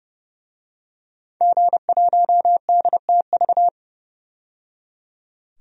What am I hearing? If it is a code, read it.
Morse code, heard as a keyed tone, about 31 wpm, 700 Hz. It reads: G1DTV
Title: DXCC-100 Call Signs - 5 Characters - 30wpm